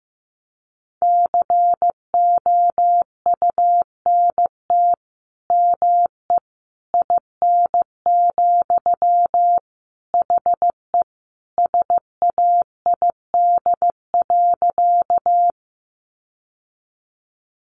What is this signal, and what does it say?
Morse code, heard as a keyed tone, about 15 wpm, 700 Hz. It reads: COUNT ME IN, HE SAID.
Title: Sentences from Top 1500 Words 15wpm